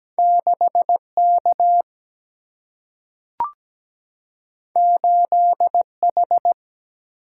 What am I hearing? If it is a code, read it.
Morse code, heard as a keyed tone, about 17 wpm, 700 Hz. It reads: 6K 8H
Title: Single-Letter Single-Number 17wpm